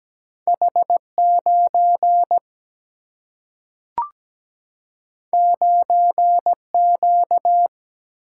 Morse code, heard as a keyed tone, about 17 wpm, 700 Hz. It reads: H 9 9 Q
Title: Single-Letter Single-Number 17wpm